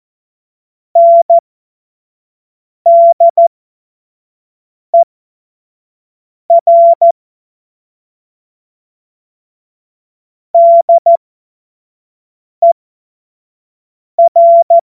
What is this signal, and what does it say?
Morse code, heard as a keyed tone, about 14 wpm, spaced as for 5 wpm, 680 Hz. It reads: NDER DER
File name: Morsekurs / Witze / Witz 21. September 2021, 25 F70